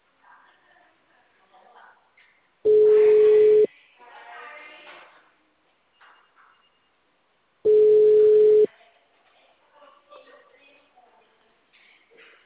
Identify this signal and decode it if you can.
Ringback tone (Europe, 425 Hz), with speech.